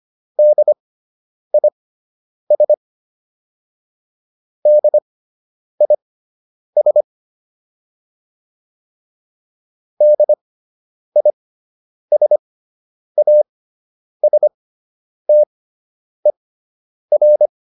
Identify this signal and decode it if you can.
Morse code, heard as a keyed tone, about 25 wpm, spaced as for 9 wpm, 600 Hz. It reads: DIS DIS DISASTER